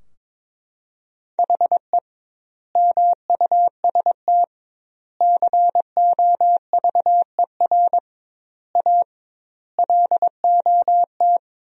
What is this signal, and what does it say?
Morse code, heard as a keyed tone, about 21 wpm, 700 Hz. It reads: HE MUST COVER A LOT
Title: 2x Word-Spacing - Sentences from Top 500 Words - 22wpm